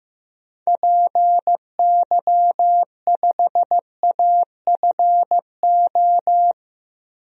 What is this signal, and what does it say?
Morse code, heard as a keyed tone, about 16 wpm, 700 Hz. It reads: PY5AFO
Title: DXCC-100 Call Signs - 6 Characters - 15wpm